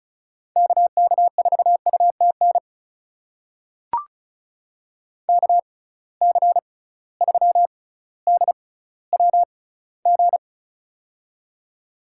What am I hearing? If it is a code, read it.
Morse code, heard as a keyed tone, about 36 wpm, spaced as for 18 wpm, 700 Hz. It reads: KK4UTN KC3DWG
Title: USA Call Signs - 6 Characters - 6x Character Spacing - 35wpm